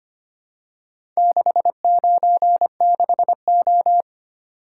Morse code, heard as a keyed tone, about 25 wpm, 700 Hz. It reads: 696O
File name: practice / 25wpm / Four Letter-Number Combinations - 25wpm